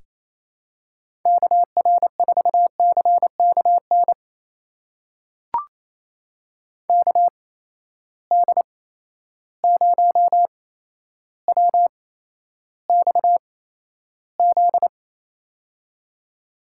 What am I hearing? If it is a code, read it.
Morse code, heard as a keyed tone, about 28 wpm, spaced as for 12 wpm, 700 Hz. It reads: KR4CKN KD0WXZ
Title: USA Call Signs - 6 Characters - 8x Character Spacing - 28wpm